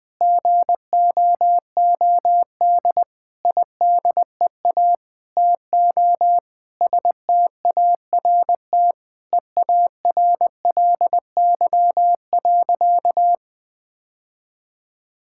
Morse code, heard as a keyed tone, about 20 wpm, 700 Hz. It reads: GOOD IDEA TO START EARLY.
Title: Sentences with 9-Word Phrases - 20wpm